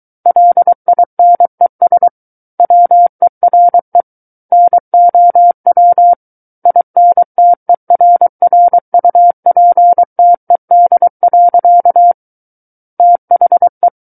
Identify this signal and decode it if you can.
Morse code, heard as a keyed tone, about 23 wpm, 700 Hz. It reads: LINES WERE NOW INTERRUPTED. THE
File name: Book One / 14-Chapter 14 - 23 WPM